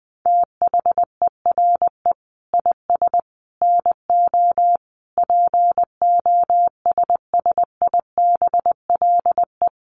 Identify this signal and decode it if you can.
Morse code, heard as a keyed tone, about 20 wpm, 700 Hz. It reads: THERE IS NO POSSIBLE
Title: Sentences from Top 500 Words 20wpm